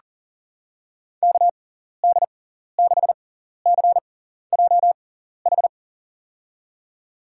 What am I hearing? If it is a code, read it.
Morse code, heard as a keyed tone, about 40 wpm, spaced as for 14 wpm, 700 Hz. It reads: KD6CJH
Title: USA Call Signs - 6 Characters - 6x Character Spacing - 40wpm